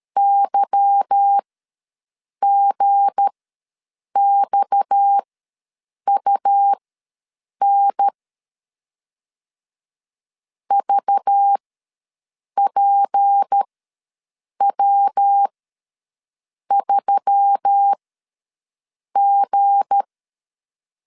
Morse code, heard as a keyed tone, about 13 wpm, spaced as for 6 wpm, 800 Hz. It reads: YGXUN VPW3G
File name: LEZIONI / 8 YJ45 / 8 Y ws 12wpm 3409